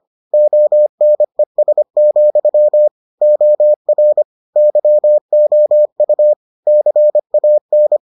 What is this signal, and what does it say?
Morse code, heard as a keyed tone, about 25 wpm, 600 Hz. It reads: ONES, OR YOU CAN